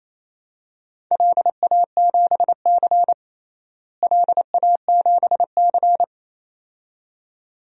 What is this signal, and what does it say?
Morse code, heard as a keyed tone, about 28 wpm, 700 Hz. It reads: LA7C LA7C
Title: DXCC-100 Call Signs - 4 Characters - Repeated 2x - 28wpm